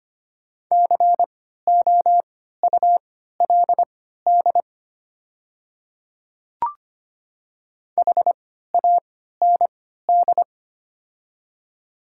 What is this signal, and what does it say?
Morse code, heard as a keyed tone, about 25 wpm, spaced as for 14 wpm, 700 Hz. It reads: COULD HAND
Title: 3x Character Spacing - Top 200 Words - 25wpm